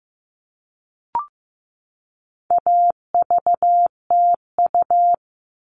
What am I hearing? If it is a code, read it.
Morse code, heard as a keyed tone, about 15 wpm, 700 Hz. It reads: AVTU
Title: All Cut Numbers - 4-Digit 15wpm